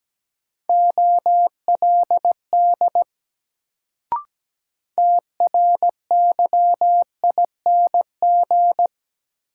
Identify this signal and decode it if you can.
Morse code, heard as a keyed tone, about 17 wpm, 700 Hz. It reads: OLD TRYING
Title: Sets of 2 Words - New Word Drill - 17wpm